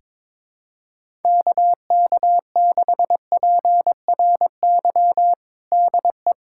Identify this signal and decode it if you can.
Morse code, heard as a keyed tone, about 22 wpm, 700 Hz. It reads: KK6PRY DE
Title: USA Callsign DE Callsign - 6 Characters - 22wpm